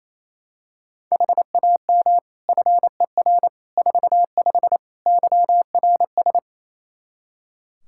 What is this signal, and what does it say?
Morse code, heard as a keyed tone, about 28 wpm, 700 Hz. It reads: HAM FER 45 YRS
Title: K1USN SST QSO Practice 28wpm